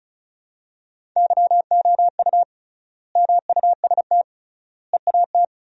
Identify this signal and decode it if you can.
Morse code, heard as a keyed tone, about 36 wpm, 700 Hz. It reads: YOU MUST EAT
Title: 3x Word-Spacing - Sentences from Top 300 Words - 35wpm